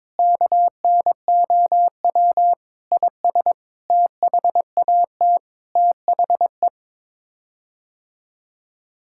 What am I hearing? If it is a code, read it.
Morse code, heard as a keyed tone, about 22 wpm, 700 Hz. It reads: KNOW IS THAT THE
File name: practice / 22wpm / Sets of 7 Words - 22wpm